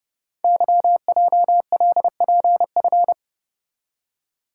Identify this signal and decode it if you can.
Morse code, heard as a keyed tone, about 30 wpm, 700 Hz. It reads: YJLPF
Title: Code Groups - 5 Letters - 30wpm